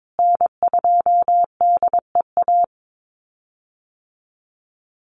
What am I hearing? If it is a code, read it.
Morse code, heard as a keyed tone, about 22 wpm, 700 Hz. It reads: N2DEA
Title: USA Call Signs Repeated 3x 22wpm